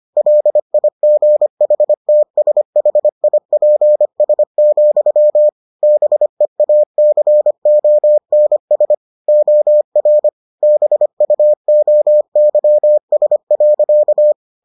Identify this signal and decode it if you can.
Morse code, heard as a keyed tone, about 25 wpm, 600 Hz. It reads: LIGHTSHIPS, BEACONS OR BUOYS.